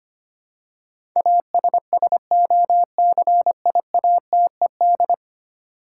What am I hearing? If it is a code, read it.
Morse code, heard as a keyed tone, about 25 wpm, 700 Hz. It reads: ASSOCIATED
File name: practice / 25wpm / Ten Letter Words - 25wpm